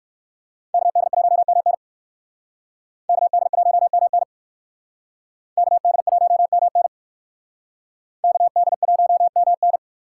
Morse code, heard as a keyed tone, about 59 wpm, 700 Hz. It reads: KD1MN KD1MN KD1MN KD1MN